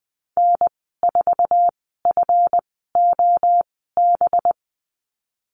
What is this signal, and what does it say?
Morse code, heard as a keyed tone, about 20 wpm, spaced as for 14 wpm, 700 Hz. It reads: N4FOB
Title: 186 Lesson 30g - Letter b - Call Signs 20wpm